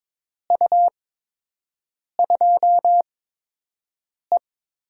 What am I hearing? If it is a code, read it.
Morse code, heard as a keyed tone, about 22 wpm, spaced as for 6 wpm, 700 Hz. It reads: U2E
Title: USA Call Signs - 4 Characters - 8x Character Spacing - 22wpm